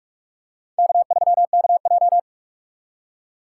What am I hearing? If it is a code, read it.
Morse code, heard as a keyed tone, about 43 wpm, 700 Hz. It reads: K3KJ